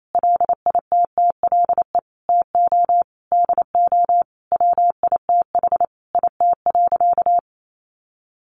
Morse code, heard as a keyed tone, about 28 wpm, 700 Hz. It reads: LITTLE TO DO WITH IT.
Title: Sentences from Top 1500 Words 28wpm